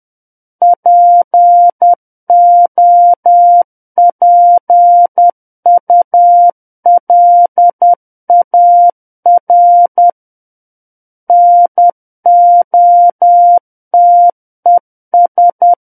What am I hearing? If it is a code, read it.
Morse code, heard as a keyed tone, about 10 wpm, 700 Hz. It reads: POPULAR NOTES
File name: Book One / 01-Chapter 1 - 10 WPM